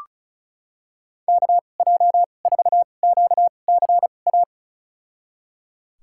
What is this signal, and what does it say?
Morse code, heard as a keyed tone, about 34 wpm, spaced as for 25 wpm, 700 Hz. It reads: KJ4QCA
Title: USA Call Signs - 6 Characters - 2x Character Spacing - 35wpm